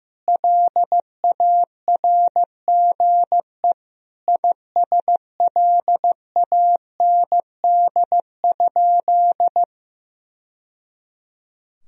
Morse code, heard as a keyed tone, about 15 wpm, 700 Hz. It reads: LARGE ISLAND?
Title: Sentences from Top 1000 Words - Review Part 3 - 15wpm